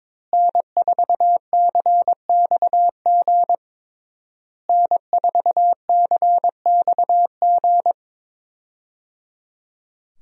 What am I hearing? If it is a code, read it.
Morse code, heard as a keyed tone, about 22 wpm, 700 Hz. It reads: N4CXG N4CXG
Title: USA Call Signs - 5 Characters - Repeated 3x - 22wpm